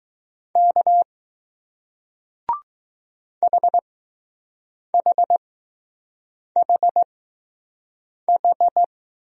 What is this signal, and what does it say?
Morse code, heard as a keyed tone, about 20 wpm, spaced as for 6 wpm, 700 Hz. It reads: K HHHH